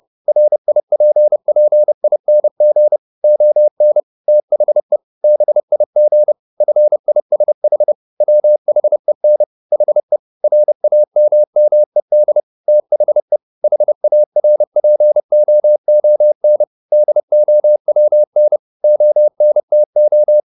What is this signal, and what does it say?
Morse code, heard as a keyed tone, about 30 wpm, 600 Hz. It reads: RIPPING ON THE BIG FISH WHEN HE RAMMED THE HARPOON DOWN ONTO